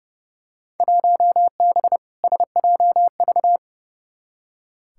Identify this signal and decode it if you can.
Morse code, heard as a keyed tone, about 29 wpm, 700 Hz. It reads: 1B SJV
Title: Field Day Exchanges - 30wpm